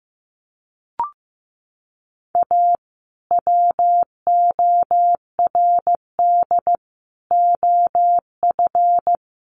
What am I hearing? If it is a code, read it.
Morse code, heard as a keyed tone, about 15 wpm, 700 Hz. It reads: A WORD OF
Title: Sentences from Top 1100 Words 15wpm